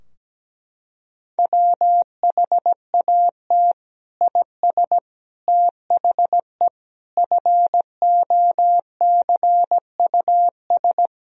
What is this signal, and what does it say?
Morse code, heard as a keyed tone, about 17 wpm, 700 Hz. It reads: WHAT IS THE FOCUS
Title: Sentences from Top 1000 Words - Review Part 4 - 17wpm